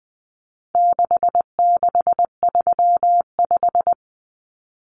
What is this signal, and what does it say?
Morse code, heard as a keyed tone, about 20 wpm, 700 Hz. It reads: 6635
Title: Numbers - 4-Digit 20wpm